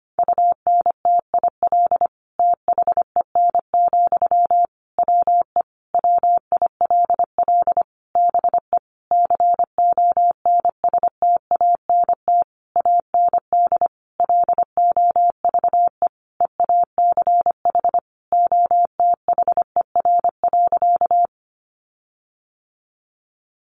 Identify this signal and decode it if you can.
Morse code, heard as a keyed tone, about 25 wpm, 700 Hz. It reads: UNTIL THEN, WE WILL BE CONSTANT AND LOVE EACH OTHER.
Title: Sentences from Top 1500 Words 25wpm